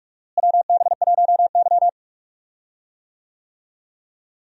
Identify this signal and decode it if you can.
Morse code, heard as a keyed tone, about 46 wpm, 700 Hz. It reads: WB1Y
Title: USA Call Signs - 4 Characters - Repeated 2x - 45wpm